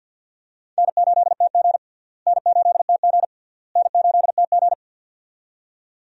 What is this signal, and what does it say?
Morse code, heard as a keyed tone, about 52 wpm, 700 Hz. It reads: N8TG N8TG N8TG
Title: USA Call Signs - 4 Characters - Repeated 3x - 50wpm